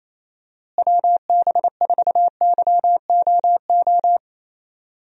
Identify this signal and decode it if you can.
Morse code, heard as a keyed tone, about 28 wpm, 700 Hz. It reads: WB4YOO
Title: USA Call Signs - 6 Characters - Repeated 3x - 28wpm